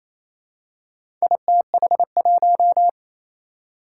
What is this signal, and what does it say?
Morse code, heard as a keyed tone, about 27 wpm, 700 Hz. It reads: ITH1
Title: Four Challenging Letter-Number Combinations 28wpm